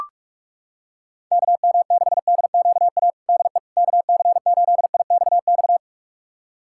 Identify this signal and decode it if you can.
Morse code, heard as a keyed tone, about 44 wpm, 700 Hz. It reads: KM6DQA DE KC8IXX
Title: USA Callsign DE Callsign 45wpm